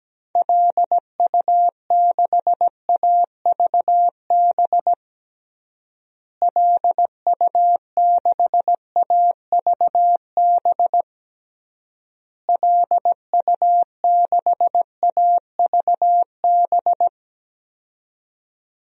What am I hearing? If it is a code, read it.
Morse code, heard as a keyed tone, about 17 wpm, 700 Hz. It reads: LU6AVB LU6AVB LU6AVB
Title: DXCC-100 Call Signs - 6 Characters - Repeated 3x - 17wpm